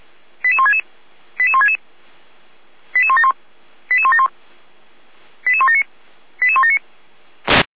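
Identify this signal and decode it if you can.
Five-tone selective call (ZVEI), heard as quick runs of five tones, tones of about 70 ms. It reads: 88166 88166 88171 88171 88179 88179
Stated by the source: SDR-received ZVEI five-tone selective call, code 88063.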